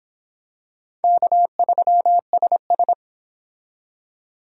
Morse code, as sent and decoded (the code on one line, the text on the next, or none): -.- ...-- ... ...
K3SS